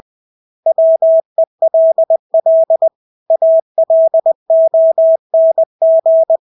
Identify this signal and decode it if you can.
Morse code, heard as a keyed tone, about 20 wpm, 650 Hz. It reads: WELL ALONG
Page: Engelske ord 20 wpm